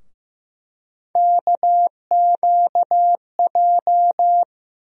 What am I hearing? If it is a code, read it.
Morse code, heard as a keyed tone, about 15 wpm, 700 Hz. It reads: KQJ
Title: USA Call Sign Postfixes - 3 Letters 15wpm